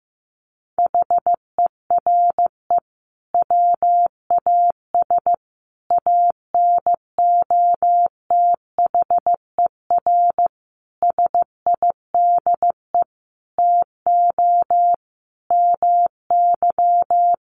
Morse code, heard as a keyed tone, about 15 wpm, 700 Hz. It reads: HERE WAS ANOTHER SIDE TO MY
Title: Sentences from Top 1500 Words 15wpm